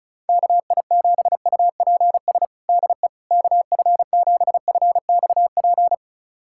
Morse code, heard as a keyed tone, about 36 wpm, 700 Hz. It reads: KI7UPS DE KF7FXP
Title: USA Callsign DE Callsign - 6 Characters - 35wpm